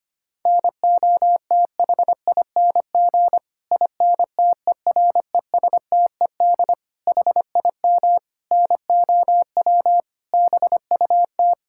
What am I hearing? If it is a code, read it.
Morse code, heard as a keyed tone, about 25 wpm, 700 Hz. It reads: NOTHING INTERESTED HIM NOW BUT
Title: Sentences from Top 1000 Words - Review Part 6 - 25wpm